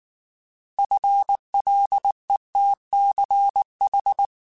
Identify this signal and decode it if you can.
Morse code, heard as a keyed tone, about 19 wpm, 800 Hz. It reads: FLETCH
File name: Alphabetical / Alphabetical mp3s / Fs @18